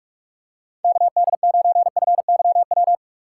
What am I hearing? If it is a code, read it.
Morse code, heard as a keyed tone, about 41 wpm, 700 Hz. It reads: KD9FYW
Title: USA Call Signs - 6 Characters - 8x Character Spacing - 45wpm